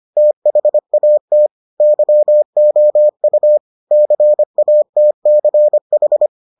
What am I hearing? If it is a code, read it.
Morse code, heard as a keyed tone, about 25 wpm, 600 Hz. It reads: THAT YOU CATCH